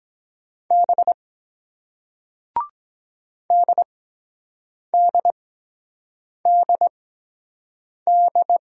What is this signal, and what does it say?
Morse code, heard as a keyed tone, about 23 wpm, spaced as for 7 wpm, 700 Hz. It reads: B DDDD